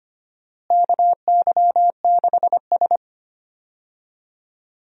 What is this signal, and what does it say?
Morse code, heard as a keyed tone, about 25 wpm, 700 Hz. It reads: KY6S
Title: USA Call Signs - 4 Characters - Repeated 2x - 25wpm